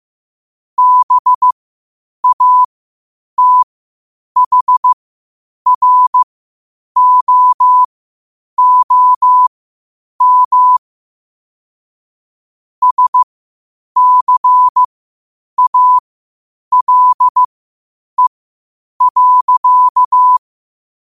Morse code, heard as a keyed tone, about 15 wpm, spaced as for 8 wpm, 1000 Hz. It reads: BATHROOM SCALE.
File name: qotdfiles / 8wpm